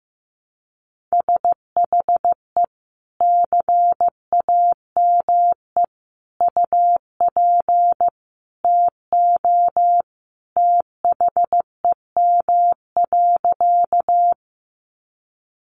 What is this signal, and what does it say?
Morse code, heard as a keyed tone, about 15 wpm, 700 Hz. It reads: SHE CAME UP TO THEM.
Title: Sentences from Top 200 Words 15wpm